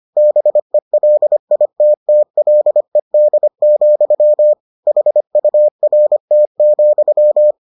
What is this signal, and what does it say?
Morse code, heard as a keyed tone, about 25 wpm, 600 Hz. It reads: BELITTLED, HURT,